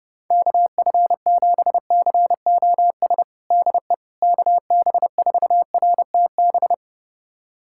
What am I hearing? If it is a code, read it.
Morse code, heard as a keyed tone, about 29 wpm, 700 Hz. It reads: KF7COS DE KB4RTB
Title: USA Callsign DE Callsign - 6 Characters - 30wpm